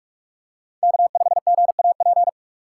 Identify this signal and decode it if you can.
Morse code, heard as a keyed tone, about 47 wpm, 700 Hz. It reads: K5GAP
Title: USA Call Signs - 2x Character Spacing - 45wpm